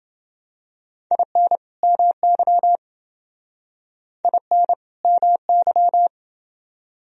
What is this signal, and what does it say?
Morse code, heard as a keyed tone, about 28 wpm, 700 Hz. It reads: IN MY IN MY